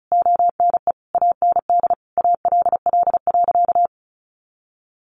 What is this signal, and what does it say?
Morse code, heard as a keyed tone, about 34 wpm, 700 Hz. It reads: ONE AND ALL.
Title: Sentences from Top 200 Words 35wpm